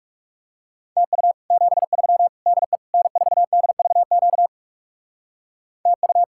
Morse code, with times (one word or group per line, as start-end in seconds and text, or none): 1.0-1.3 TU
1.5-2.3 73
2.5-2.8 DE
2.9-4.5 N4DVQ
5.9-6.2 TU